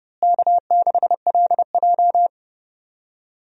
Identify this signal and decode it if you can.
Morse code, heard as a keyed tone, about 30 wpm, 700 Hz. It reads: K6LJ
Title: USA Call Signs - 4 Characters - Repeated 3x - 30wpm